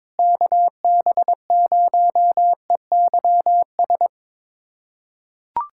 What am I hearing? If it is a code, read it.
Morse code, heard as a keyed tone, about 22 wpm, 700 Hz. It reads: KB0EYS
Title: USA Callsign DE Callsign - 6 Characters - 22wpm